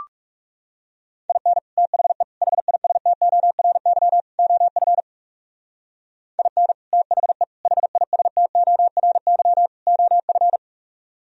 Morse code, heard as a keyed tone, about 37 wpm, 700 Hz. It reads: IN THE HISTORY OF IN THE HISTORY OF
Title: Speed-Racing - Sets of 4 Words 30wpm